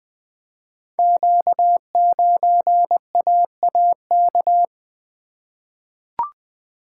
Morse code, heard as a keyed tone, about 20 wpm, 700 Hz. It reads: Q9AAK
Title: Code Groups - 5 Letter:Numbers - 20wpm